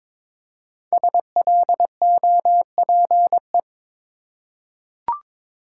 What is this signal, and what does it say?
Morse code, heard as a keyed tone, about 22 wpm, 700 Hz. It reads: SLOPE
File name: 5 Letter Words - First 3 Letters Given - 22wpm